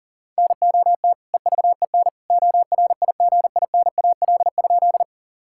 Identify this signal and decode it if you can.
Morse code, heard as a keyed tone, about 40 wpm, 700 Hz. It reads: NOT EVEN ORIGINAL?
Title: Sentences from Top 1000 Words - Review Part 6 - 40wpm